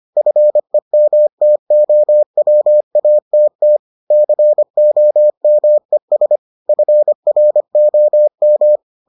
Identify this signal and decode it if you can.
Morse code, heard as a keyed tone, about 25 wpm, 600 Hz. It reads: FEMTOWATT COMES FROM